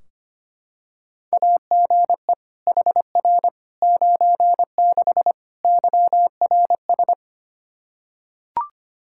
Morse code, heard as a keyed tone, about 25 wpm, 700 Hz. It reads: AGE HR 96 YRS
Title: Practice QSO - Exchange 3 - More Info - 25wpm